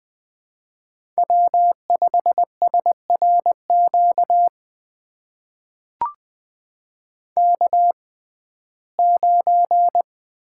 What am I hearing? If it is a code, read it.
Morse code, heard as a keyed tone, about 20 wpm, spaced as for 15 wpm, 700 Hz. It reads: W5SRQ K9
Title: USA Call Signs - 5 Characters - 6x Character Spacing - 20wpm